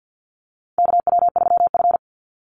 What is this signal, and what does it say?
Morse code, heard as a keyed tone, about 50 wpm, 700 Hz. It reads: KW3F
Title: Call Signs - 50wpm